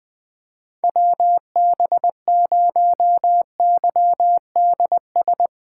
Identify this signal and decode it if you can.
Morse code, heard as a keyed tone, about 20 wpm, 700 Hz. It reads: WB0YDS
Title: USA Callsign DE Callsign - 6 Characters - 20wpm